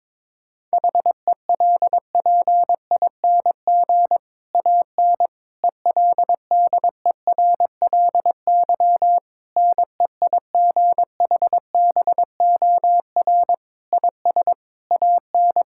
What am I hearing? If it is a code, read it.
Morse code, heard as a keyed tone, about 22 wpm, 700 Hz. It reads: HELPING AN ELDERLY NEIGHBOR IS AN
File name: Difficult Sentences with Hints - 22wpm